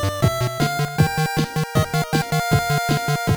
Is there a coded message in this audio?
No.